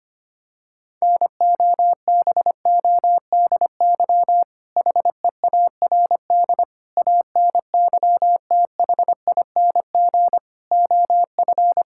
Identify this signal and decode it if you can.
Morse code, heard as a keyed tone, about 25 wpm, 700 Hz. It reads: NOBODY HEARD ANYTHING OF
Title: Sentences from Top 1600 Words 25wpm